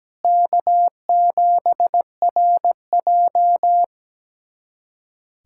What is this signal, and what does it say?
Morse code, heard as a keyed tone, about 17 wpm, 700 Hz. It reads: K7RJ
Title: USA Callsign DE Callsign - 4 Characters - 17wpm